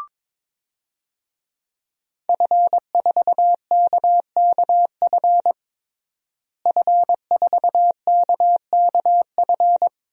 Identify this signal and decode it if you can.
Morse code, heard as a keyed tone, about 22 wpm, 700 Hz. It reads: F4KKF F4KKF
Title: DXCC-100 Call Signs - 5 Characters - Repeated 2x - 22wpm